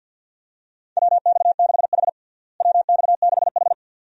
Morse code, heard as a keyed tone, about 51 wpm, 700 Hz. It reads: WX6H WX6H
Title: USA Call Signs - 4 Characters - Repeated 2x - 50wpm